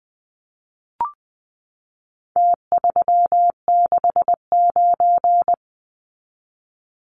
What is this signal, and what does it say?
Morse code, heard as a keyed tone, about 20 wpm, 700 Hz. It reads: T369
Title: Numbers - Contest Style - 4-Digit 20wpm